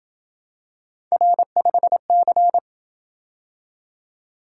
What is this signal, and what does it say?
Morse code, heard as a keyed tone, about 27 wpm, 700 Hz. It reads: R5C